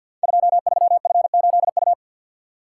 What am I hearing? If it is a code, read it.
Morse code, heard as a keyed tone, about 52 wpm, 700 Hz. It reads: 23F8U
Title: Code Groups - 5 Letter:Numbers - 50wpm